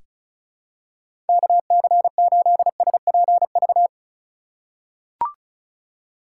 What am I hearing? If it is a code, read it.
Morse code, heard as a keyed tone, about 36 wpm, 700 Hz. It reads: KC8SPV
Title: USA Call Signs - 3x Character Spacing - 35wpm